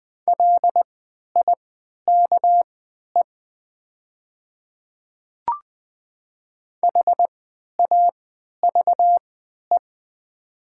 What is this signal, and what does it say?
Morse code, heard as a keyed tone, about 20 wpm, spaced as for 11 wpm, 700 Hz. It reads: LIKE HAVE
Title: 3x Character Spacing - Top 100 Words - 20wpm